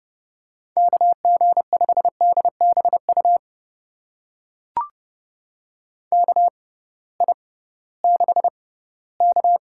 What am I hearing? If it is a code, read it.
Morse code, heard as a keyed tone, about 30 wpm, spaced as for 18 wpm, 700 Hz. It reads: KG5DBU KI6K
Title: USA Call Signs - 6 Characters - 6x Character Spacing - 30wpm